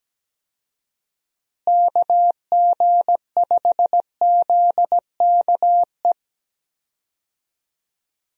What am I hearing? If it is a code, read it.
Morse code, heard as a keyed tone, about 17 wpm, 700 Hz. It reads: KG5ZKE
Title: USA Call Signs - 6 Characters - Repeated 2x - 17wpm